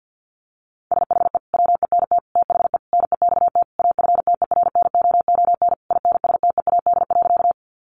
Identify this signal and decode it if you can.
Morse code, heard as a keyed tone, about 50 wpm, 700 Hz. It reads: SHE WENT THE NEXT AFTERNOON INSTEAD.
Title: Sentences from Top 1500 Words 50wpm